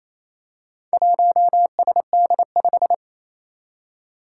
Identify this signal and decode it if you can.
Morse code, heard as a keyed tone, about 29 wpm, 700 Hz. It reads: 1SD5
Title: Four Challenging Letter-Number Combinations 28wpm